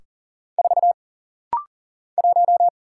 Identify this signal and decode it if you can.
Morse code, heard as a keyed tone, about 40 wpm, 700 Hz. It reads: 4 1
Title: Single Letter-Number - Rapid-Fire 40wpm